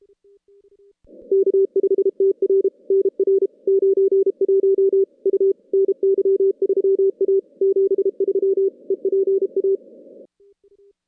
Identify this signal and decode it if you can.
Morse code, heard as a keyed tone, about 32 wpm, 390 Hz. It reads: K5TR NR 91 U NY3A 73 EPA
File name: contests / CW-SS-k5tr- -2200